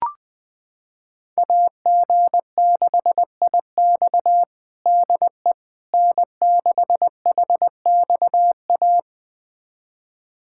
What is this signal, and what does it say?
Morse code, heard as a keyed tone, about 20 wpm, 700 Hz. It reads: AG6IX DE N6HXA
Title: USA Callsign DE Callsign - 5 Characters - 20wpm